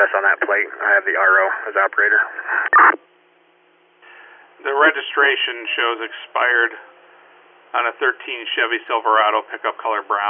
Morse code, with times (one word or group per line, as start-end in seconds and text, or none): none